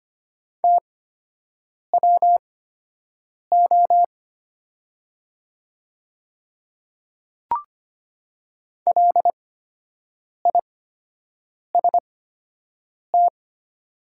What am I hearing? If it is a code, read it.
Morse code, heard as a keyed tone, about 25 wpm, spaced as for 7 wpm, 690 Hz. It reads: TWO LIST
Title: 8x Character Spacing - Top 500 Words - 25wpm